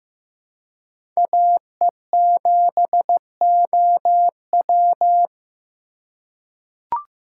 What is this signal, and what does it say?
Morse code, heard as a keyed tone, about 15 wpm, 700 Hz. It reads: AE7OW
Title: USA Call Signs - 3x Character Spacing - 15wpm